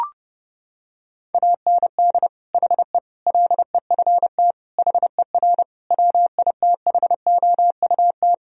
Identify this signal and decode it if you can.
Morse code, heard as a keyed tone, about 30 wpm, 700 Hz. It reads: AND HE LEFT HER WITHOUT
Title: Sentences from Top 1000 Words - Review Part 3 - 30wpm